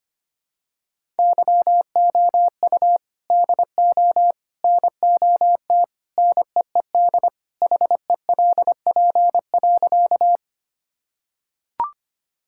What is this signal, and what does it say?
Morse code, heard as a keyed tone, about 25 wpm, 700 Hz. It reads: YOU DO NOT NEED HELP.
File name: practice / 25wpm / Sentences with 3-Word Phrases - 25wpm